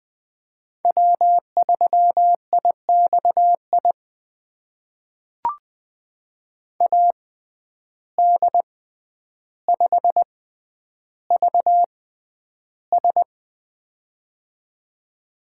Morse code, heard as a keyed tone, about 20 wpm, spaced as for 10 wpm, 700 Hz. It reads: W3IXI AD5VS
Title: USA Call Signs - 5 Characters - 6x Character Spacing - 20wpm